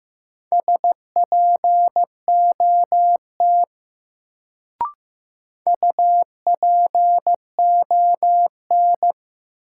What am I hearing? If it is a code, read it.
Morse code, heard as a keyed tone, about 15 wpm, 700 Hz. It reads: SPOT UPON
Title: Sentences from Top 1100 Words - New Word Drill Alt - 15wpm